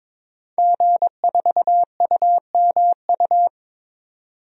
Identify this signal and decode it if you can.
Morse code, heard as a keyed tone, about 21 wpm, 700 Hz. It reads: G4UMU
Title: DXCC-100 Call Signs - 5 Characters - 22wpm